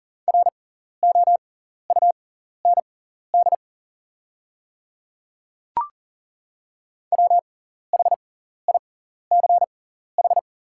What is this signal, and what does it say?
Morse code, heard as a keyed tone, about 38 wpm, spaced as for 14 wpm, 690 Hz. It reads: ROUND WHICH
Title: 6x Character Spacing - Top 200 Words - 40wpm